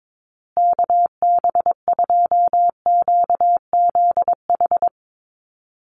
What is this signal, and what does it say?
Morse code, heard as a keyed tone, about 22 wpm, 700 Hz. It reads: KB2QZH
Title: USA Call Signs Repeated 2x 22wpm